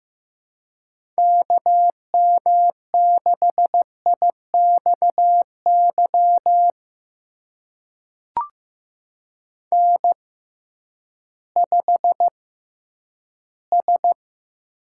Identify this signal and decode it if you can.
Morse code, heard as a keyed tone, about 15 wpm, spaced as for 10 wpm, 700 Hz. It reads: KM6IXY N5S
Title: USA Call Signs - 6x Character Spacing - 15wpm